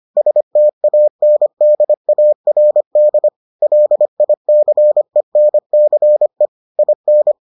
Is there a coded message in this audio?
Yes: Morse code from 0.2 to 7.3 s, heard as a keyed tone, about 25 wpm, 600 Hz. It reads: STANDARD LICENCE IN